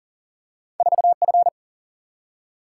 Morse code, heard as a keyed tone, about 40 wpm, 700 Hz. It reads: 4F
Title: Single-Letter Single-Number 40wpm